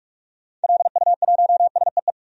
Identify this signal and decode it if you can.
Morse code, heard as a keyed tone, about 41 wpm, 700 Hz. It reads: LU1SEE
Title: DXCC-100 Call Signs - 6 Characters - 45wpm